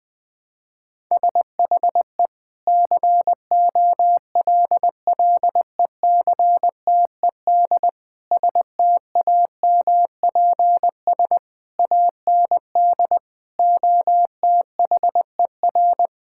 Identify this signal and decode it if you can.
Morse code, heard as a keyed tone, about 20 wpm, 700 Hz. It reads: SHE COLLECTED STAMPS AND OTHER
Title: Sentences with 2-Word Phrases - 20wpm